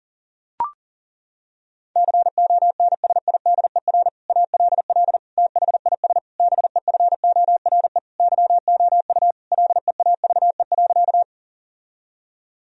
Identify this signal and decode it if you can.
Morse code, heard as a keyed tone, about 37 wpm, 700 Hz. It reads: CONSIDER ALL THIS BEFORE YOU LEAVE.
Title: Sentences from Top 1000 Words - Review Part 4 - 40wpm